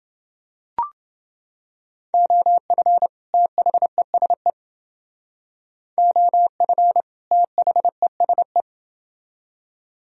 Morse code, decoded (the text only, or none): OF THESE OF THESE